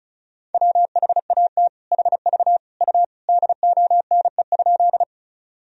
Morse code, heard as a keyed tone, about 34 wpm, 700 Hz. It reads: WHAT HV U DONE?
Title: Sentences from Top 500 Words and Abbreviations 35wpm